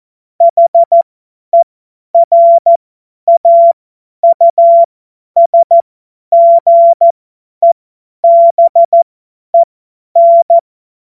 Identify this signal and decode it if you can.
Morse code, heard as a keyed tone, about 14 wpm, spaced as for 10 wpm, 680 Hz. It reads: HERAUSGEBEN
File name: Morsekurs / Witze / Witz 21.November 2023 50 F70